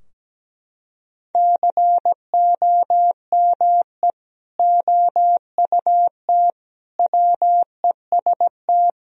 Morse code, heard as a keyed tone, about 17 wpm, 700 Hz. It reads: COME OUT WEST
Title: Sentences from Top 1000 Words - Review Part 2 - 17wpm